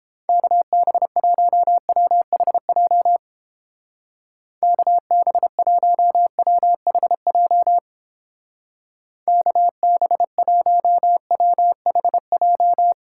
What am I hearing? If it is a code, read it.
Morse code, heard as a keyed tone, about 31 wpm, 700 Hz. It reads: KB1WHJ KB1WHJ KB1WHJ